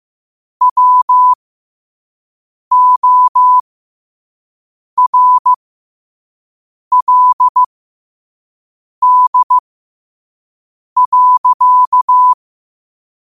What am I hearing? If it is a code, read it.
Morse code, heard as a keyed tone, about 15 wpm, spaced as for 5 wpm, 1000 Hz. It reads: WORLD.
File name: qotdfiles / 5wpm